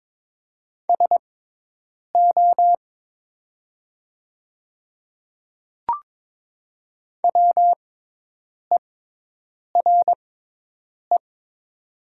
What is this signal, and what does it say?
Morse code, heard as a keyed tone, about 22 wpm, spaced as for 8 wpm, 700 Hz. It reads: SO WERE
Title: 6x Character Spacing - Top 100 Words - 22wpm